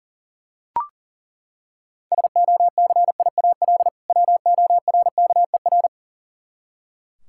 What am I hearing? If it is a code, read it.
Morse code, heard as a keyed tone, about 40 wpm, 700 Hz. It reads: SOCIAL WORKER
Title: Practice QSO - Exchange 3 - More Info - 40wpm